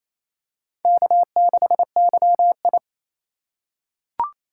Morse code, heard as a keyed tone, about 28 wpm, 700 Hz. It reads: K6YI
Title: USA Call Signs - 4 Characters - 4x Character Spacing - 28wpm